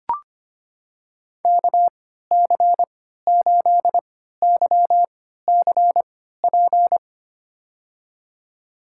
Morse code, heard as a keyed tone, about 25 wpm, spaced as for 14 wpm, 700 Hz. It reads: KC8YCP
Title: USA Call Signs - 6 Characters - 3x Character Spacing - 25wpm